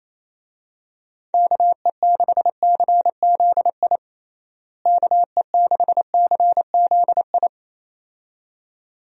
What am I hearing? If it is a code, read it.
Morse code, heard as a keyed tone, about 28 wpm, 700 Hz. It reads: KE6CZI KE6CZI
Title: USA Call Signs - 6 Characters - Repeated 2x - 28wpm